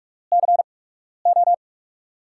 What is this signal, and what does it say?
Morse code, heard as a keyed tone, about 44 wpm, spaced as for 12 wpm, 700 Hz. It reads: CO
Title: USA Call Signs - 5 Characters - 8x Character Spacing - 45wpm